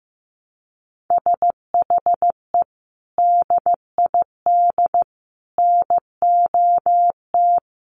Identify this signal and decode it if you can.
Morse code, heard as a keyed tone, about 15 wpm, 700 Hz. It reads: SHE DID NOT
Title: Sentences from Top 1100 Words 15wpm